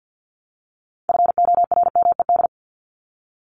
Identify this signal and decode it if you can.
Morse code, heard as a keyed tone, about 50 wpm, 700 Hz. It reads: FORMED
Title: ED Suffix - Top 100 Words with Base Words 50wpm